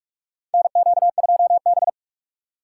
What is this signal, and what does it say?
Morse code, heard as a keyed tone, about 47 wpm, 700 Hz. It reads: NQ2B
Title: USA Call Signs - 4 Characters - 2x Character Spacing - 45wpm